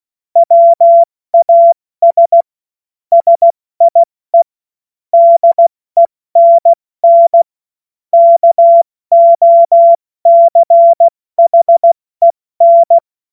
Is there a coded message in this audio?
Yes: Morse code from 0.4 to 13.0 s, heard as a keyed tone, about 16 wpm, 680 Hz. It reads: WAS SIE DENN KOCHEN